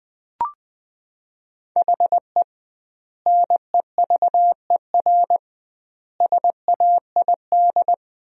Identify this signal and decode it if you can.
Morse code, heard as a keyed tone, about 20 wpm, 700 Hz. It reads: HE NEVER SAID
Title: 2x Word-Spacing - Sentences from Top 500 Words - 20wpm